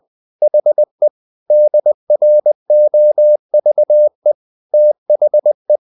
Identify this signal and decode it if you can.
Morse code, heard as a keyed tone, about 20 wpm, 600 Hz. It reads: HE DROVE THE
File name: the old man and the sea part 2 20wpm